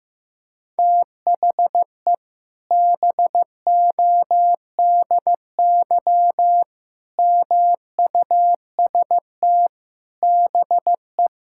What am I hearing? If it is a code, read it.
Morse code, heard as a keyed tone, about 16 wpm, 700 Hz. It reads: THE BODY MUST BE
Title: Sentences from Top 1000 Words - Review Part 5 - 15wpm